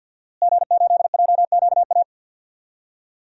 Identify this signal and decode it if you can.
Morse code, heard as a keyed tone, about 51 wpm, 700 Hz. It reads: G8JQA
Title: DXCC-100 Call Signs - 5 Characters - Repeated 3x - 50wpm